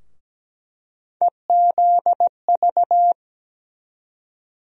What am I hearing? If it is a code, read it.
Morse code, heard as a keyed tone, about 17 wpm, 700 Hz. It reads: EZV
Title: USA Call Sign Postfixes - 3 Letters 17wpm